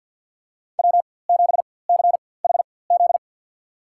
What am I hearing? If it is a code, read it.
Morse code, heard as a keyed tone, about 50 wpm, spaced as for 23 wpm, 700 Hz. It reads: W7CHZ
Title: USA Call Signs - 5 Characters - 4x Character Spacing - 50wpm